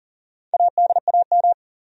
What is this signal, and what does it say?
Morse code, heard as a keyed tone, about 40 wpm, 700 Hz. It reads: ADAM
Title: USA Common Names - 40wpm